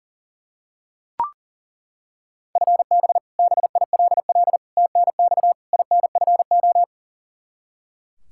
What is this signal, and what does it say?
Morse code, heard as a keyed tone, about 40 wpm, 700 Hz. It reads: FB BILL TNX INFO
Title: Practice QSO - Exchange 3 - More Info - 40wpm